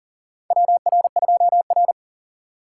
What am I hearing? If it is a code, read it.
Morse code, heard as a keyed tone, about 40 wpm, 700 Hz. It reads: WR2R
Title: USA Call Signs - 4 Characters - 3x Character Spacing - 40wpm